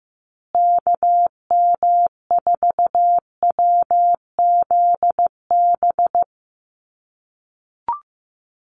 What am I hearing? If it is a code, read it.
Morse code, heard as a keyed tone, about 15 wpm, 700 Hz. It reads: KM4WZB
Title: USA Call Signs Repeated 3x 15wpm